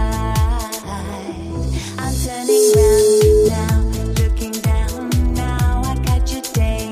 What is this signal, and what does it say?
Ringback tone (Europe, 425 Hz), with noise.